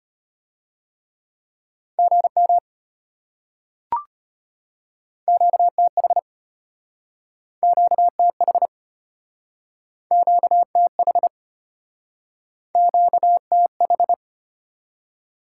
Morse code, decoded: GM QTH QTH QTH QTH

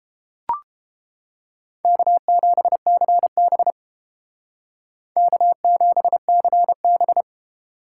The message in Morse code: K7CB K7CB

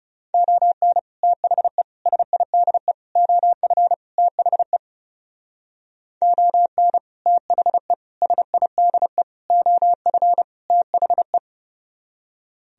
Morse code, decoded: ON THE SIDE OF THE ON THE SIDE OF THE

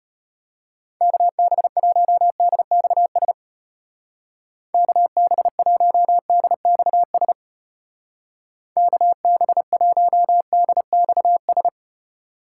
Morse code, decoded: KB1DXS KB1DXS KB1DXS

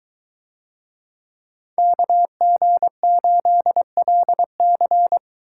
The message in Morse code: KG8LC

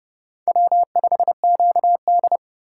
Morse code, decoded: W5QD

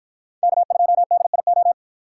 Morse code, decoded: K2DIO

